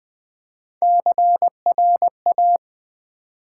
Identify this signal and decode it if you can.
Morse code, heard as a keyed tone, about 20 wpm, 700 Hz. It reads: CRA
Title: Build A Word - 1 Letter At A Time - 5-Letter Words - 20wpm